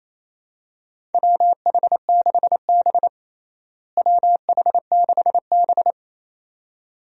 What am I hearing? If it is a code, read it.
Morse code, heard as a keyed tone, about 28 wpm, 700 Hz. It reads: WH6B WH6B
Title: USA Call Signs - 4 Characters - Repeated 2x - 28wpm